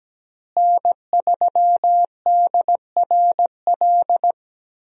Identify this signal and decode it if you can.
Morse code, heard as a keyed tone, about 17 wpm, 700 Hz. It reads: N3DRL
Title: USA Call Signs - 5 Characters - Repeated 2x - 17wpm